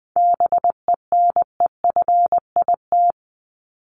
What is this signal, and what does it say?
Morse code, heard as a keyed tone, about 20 wpm, 700 Hz. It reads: BENEFIT
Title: Top 751-1000 Words 20wpm